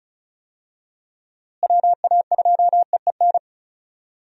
Morse code, heard as a keyed tone, about 36 wpm, 700 Hz. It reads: WA2EEN
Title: USA Call Signs - 6 Characters - Repeated 3x - 35wpm